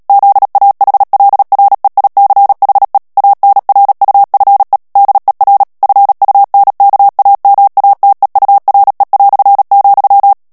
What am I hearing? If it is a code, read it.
Morse code, heard as a keyed tone, about 37 wpm, 800 Hz. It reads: ZAHLREICHE ANRUFE DER FUNKAMATEURE",
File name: morse mp3 / ALLEMAND texte clair 2 vitesse 35 mots minute